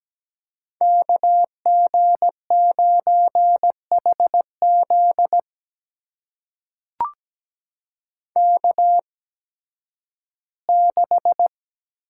Morse code, heard as a keyed tone, about 17 wpm, spaced as for 11 wpm, 700 Hz. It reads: KG9HZ K6
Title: USA Call Signs - 5 Characters - 8x Character Spacing - 17wpm